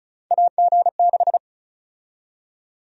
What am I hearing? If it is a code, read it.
Morse code, heard as a keyed tone, about 34 wpm, 700 Hz. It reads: AG6
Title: USA Call Sign Prefixes - 35wpm